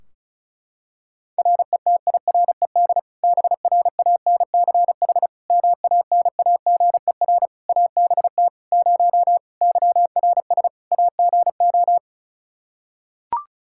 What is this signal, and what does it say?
Morse code, heard as a keyed tone, about 36 wpm, 700 Hz. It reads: RETIRED BRANCH MANAGER ABT 0 YRS AGO
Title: Practice QSO - Exchange 3 - More Info - 35wpm